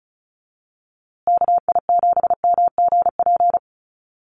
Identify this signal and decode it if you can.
Morse code, heard as a keyed tone, about 34 wpm, 700 Hz. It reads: KI7MGP
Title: USA Call Signs Repeated 2x 35wpm